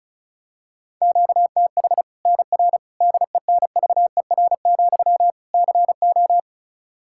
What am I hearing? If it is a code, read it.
Morse code, heard as a keyed tone, about 34 wpm, 700 Hz. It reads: QTH NR DENVER, CO
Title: K1USN SST QSO Practice 35wpm